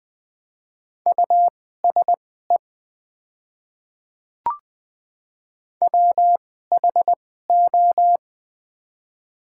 Morse code, heard as a keyed tone, about 20 wpm, spaced as for 14 wpm, 700 Hz. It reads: USE WHO
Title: 2x Character Spacing - Top 100 Words - 20wpm